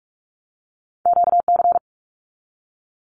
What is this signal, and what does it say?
Morse code, heard as a keyed tone, about 46 wpm, 700 Hz. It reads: QC
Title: Canadian Provinces and Territories 45wpm